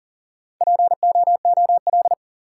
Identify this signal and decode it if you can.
Morse code, heard as a keyed tone, about 37 wpm, 700 Hz. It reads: POOL
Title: Four Letter Words - 40wpm